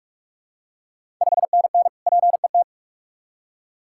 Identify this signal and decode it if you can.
Morse code, heard as a keyed tone, about 46 wpm, 690 Hz. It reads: 5NN PET
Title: Ontario QSO Party Exchange 45wpm